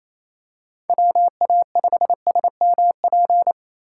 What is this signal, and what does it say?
Morse code, heard as a keyed tone, about 28 wpm, 700 Hz. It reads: WA5SMP
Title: USA Call Signs - 6 Characters - 4x Character Spacing - 28wpm